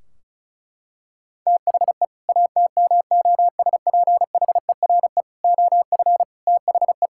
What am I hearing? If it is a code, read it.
Morse code, heard as a keyed tone, about 36 wpm, 690 Hz. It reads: THE ATMOSPHERE OF THE
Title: Sentences from Top 1600 Words 35wpm